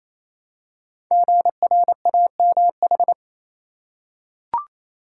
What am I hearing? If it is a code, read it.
Morse code, heard as a keyed tone, about 27 wpm, 700 Hz. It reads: GRAMH
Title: Code Groups - 5 Letters - 28wpm